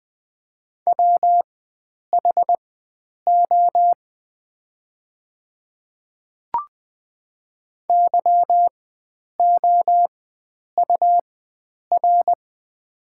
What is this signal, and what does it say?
Morse code, heard as a keyed tone, about 20 wpm, spaced as for 9 wpm, 700 Hz. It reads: WHO YOUR
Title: 4x Character Spacing - Top 100 Words - 20wpm